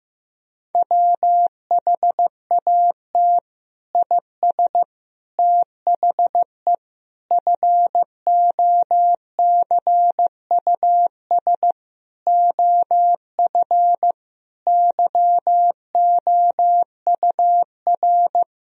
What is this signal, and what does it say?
Morse code, heard as a keyed tone, about 15 wpm, 700 Hz. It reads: WHAT IS THE FOCUS OF YOUR
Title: Sentences from Top 1000 Words - Review Part 4 - 15wpm